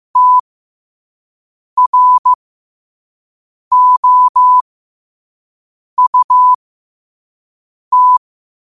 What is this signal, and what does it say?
Morse code, heard as a keyed tone, about 15 wpm, spaced as for 5 wpm, 1000 Hz. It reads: TROUT
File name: qotdfiles / 5wpm